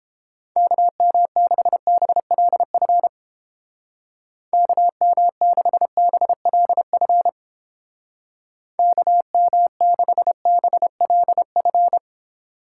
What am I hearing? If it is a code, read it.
Morse code, heard as a keyed tone, about 30 wpm, 700 Hz. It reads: KM6BLF KM6BLF KM6BLF